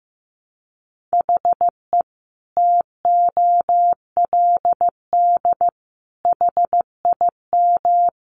Morse code, heard as a keyed tone, about 15 wpm, 700 Hz. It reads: HE TOLD HIM
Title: Sentences from Top 700 Words 15wpm